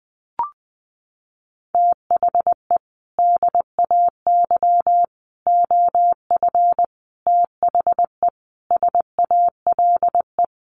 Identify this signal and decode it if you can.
Morse code, heard as a keyed tone, about 20 wpm, 700 Hz. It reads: THE DAY OF THE SALE
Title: Sentences from Top 1400 Words 20wpm